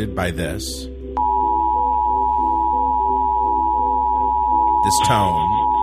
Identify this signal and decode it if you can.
Emergency Alert System (SAME) tones (the 853 and 960 Hz attention tone), with speech. It is heard from 1.1 to 5.1 s.